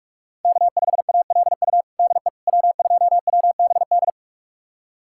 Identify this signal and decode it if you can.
Morse code, heard as a keyed tone, about 41 wpm, 700 Hz. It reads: K5ALU DE W2WBD
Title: USA Callsign DE Callsign - 5 Characters - 45wpm